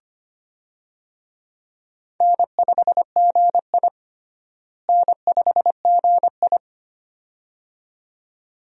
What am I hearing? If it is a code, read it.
Morse code, heard as a keyed tone, about 25 wpm, 700 Hz. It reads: N5GI N5GI
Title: USA Call Signs - 4 Characters - Repeated 2x - 25wpm